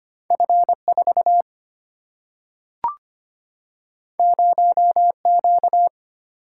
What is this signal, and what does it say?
Morse code, heard as a keyed tone, about 25 wpm, 700 Hz. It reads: F 4 0 Q
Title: Single-Letter Single-Number 25wpm